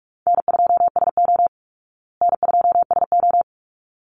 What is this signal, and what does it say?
Morse code, heard as a keyed tone, about 44 wpm, 700 Hz. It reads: N2SO N2SO
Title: Easy Call Signs Repeated 2x - 45wpm